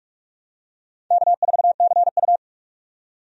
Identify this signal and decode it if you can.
Morse code, heard as a keyed tone, about 46 wpm, 700 Hz. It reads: K4CU
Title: USA Call Signs - 4 Characters - 45wpm